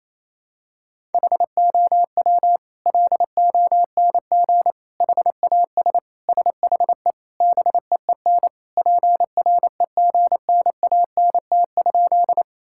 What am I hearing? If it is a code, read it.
Morse code, heard as a keyed tone, about 27 wpm, 690 Hz. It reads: HOW LONG HAS SHE BEEN PREGNANT?
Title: Sentences from Top 1000 Words - Review Part 6 - 28wpm